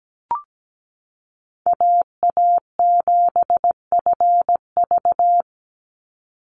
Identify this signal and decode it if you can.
Morse code, heard as a keyed tone, about 17 wpm, 700 Hz. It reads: AA7FV
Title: Call Signs - 17wpm